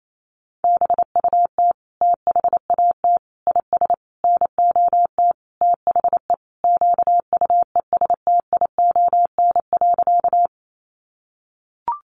Morse code, heard as a keyed tone, about 28 wpm, 700 Hz. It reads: BUT THAT IS NOT THE QUESTION.
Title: Sentences from Top 600 Words 28wpm